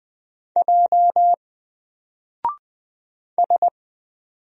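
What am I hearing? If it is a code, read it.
Morse code, heard as a keyed tone, about 20 wpm, 700 Hz. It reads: J S
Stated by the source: Single letter and number Morse drill at 20 wpm, from N4LCD.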